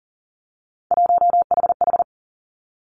Morse code, heard as a keyed tone, about 42 wpm, 690 Hz. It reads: 1HH
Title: Three Challenging Letter-Number Combinations 40wpm